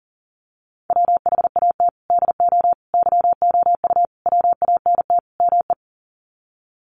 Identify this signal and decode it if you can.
Morse code, heard as a keyed tone, about 40 wpm, 700 Hz. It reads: WHAT DO YOU WANT ME
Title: Sets of 5 Words Encore2 40wpm